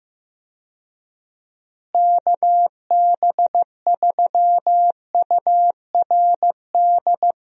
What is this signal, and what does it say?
Morse code, heard as a keyed tone, about 15 wpm, 700 Hz. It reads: KB3URD
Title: USA Call Signs - 6 Characters - Repeated 3x - 15wpm